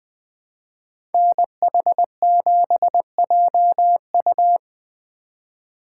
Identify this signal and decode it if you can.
Morse code, heard as a keyed tone, about 21 wpm, 700 Hz. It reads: NH7JU
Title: Speed Racing - USA Call Signs 17wpm